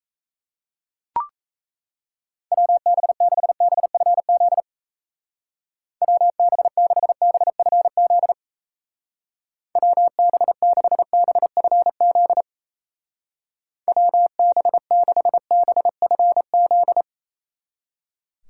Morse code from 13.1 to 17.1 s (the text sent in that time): WB6BFZ